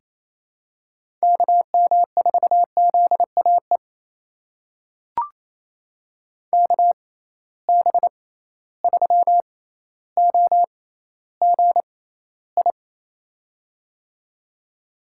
Morse code, heard as a keyed tone, about 28 wpm, spaced as for 14 wpm, 700 Hz. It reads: KM4ZAE KB3OGI
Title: USA Call Signs - 6 Characters - 6x Character Spacing - 28wpm